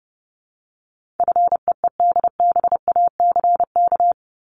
Morse code, heard as a keyed tone, about 30 wpm, 700 Hz. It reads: FEEDBACK
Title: Top 1501-1750 Words 30wpm